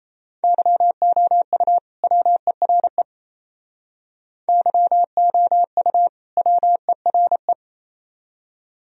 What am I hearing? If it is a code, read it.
Morse code, heard as a keyed tone, about 35 wpm, 700 Hz. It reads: YOU WERE YOU WERE
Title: Speed-Racing - Sets of 2 Words 28wpm